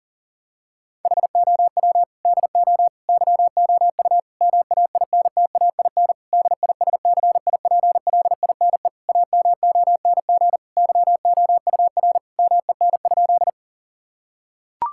Morse code, heard as a keyed tone, about 40 wpm, 700 Hz. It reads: HOW DO YOU MAINTAIN DISCIPLINE AMONG YOUR MEN?
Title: Sentences from Top 1000 Words - Review Part 6 - 40wpm